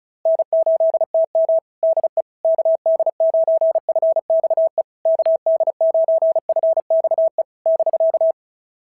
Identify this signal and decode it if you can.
Morse code, heard as a keyed tone, about 36 wpm, 690 Hz. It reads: N8TM DE KD9FXE KD9FXE <BK>
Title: SOTA QSO Practice 35wpm